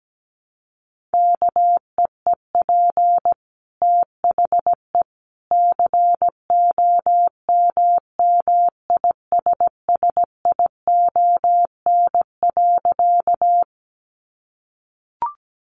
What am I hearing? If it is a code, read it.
Morse code, heard as a keyed tone, about 17 wpm, 700 Hz. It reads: KEEP THE COMMISSION.
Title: Sentences from Top 1400 Words 17wpm